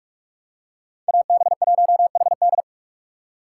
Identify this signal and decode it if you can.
Morse code, heard as a keyed tone, about 45 wpm, 700 Hz. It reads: AB1HD